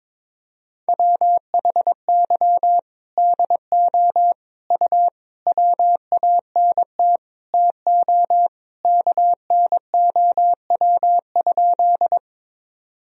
Morse code, heard as a keyed tone, about 23 wpm, 700 Hz. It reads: WHY DO U WANT TO KNOW?
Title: Sentences from Top 200 Words and Abbreviations 22wpm